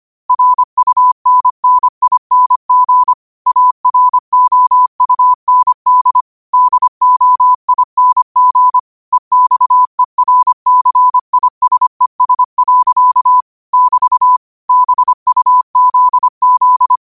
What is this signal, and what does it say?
Morse code, heard as a keyed tone, about 25 wpm, 1000 Hz. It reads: RUNNING AROUND DOING EXERCISES. <BT> BUZZ
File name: qotdfiles / 25wpm